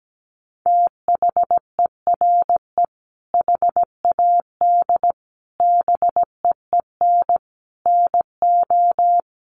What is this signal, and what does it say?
Morse code, heard as a keyed tone, about 17 wpm, 700 Hz. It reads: THERE HAD BEEN NO
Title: Sentences from Top 1400 Words 17wpm